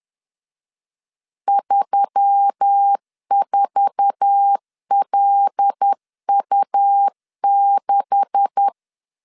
Morse code, heard as a keyed tone, about 10 wpm, 800 Hz. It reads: 34LU6
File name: CW / DET / 1 9 1500C S B 10wpm 4439